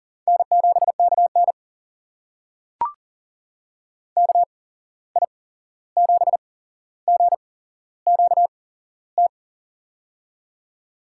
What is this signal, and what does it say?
Morse code, heard as a keyed tone, about 39 wpm, spaced as for 15 wpm, 700 Hz. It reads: N7KN KI7GQT
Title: USA Call Signs - 8x Character Spacing - 40wpm